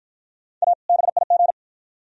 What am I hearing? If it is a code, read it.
Morse code, heard as a keyed tone, about 54 wpm, 700 Hz. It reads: A BIG